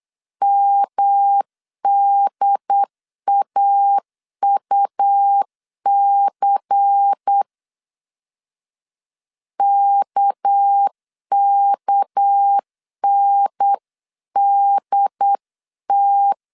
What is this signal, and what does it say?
Morse code, heard as a keyed tone, about 8 wpm, 800 Hz. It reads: MDAUC KKNDT
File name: CW / DET / 1 6 1500C S B 8wpm 5142